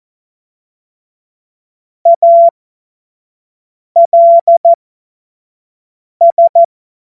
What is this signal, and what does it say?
Morse code, heard as a keyed tone, about 14 wpm, spaced as for 5 wpm, 680 Hz. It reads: ALS
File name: Morsekurs / Witze / Witz 10. Januar 2023 25 F70